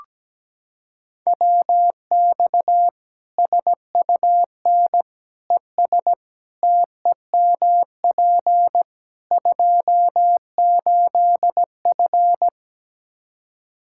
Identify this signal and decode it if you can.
Morse code, heard as a keyed tone, about 17 wpm, 700 Hz. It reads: WX SUN ES TEMP 28F
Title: Practice QSO - Exchange 2 - Info - 17wpm